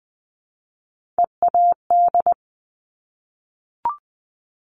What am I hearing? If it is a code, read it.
Morse code, heard as a keyed tone, about 20 wpm, 700 Hz. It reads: EAD
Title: Top Three Letter Combinations in Words 20wpm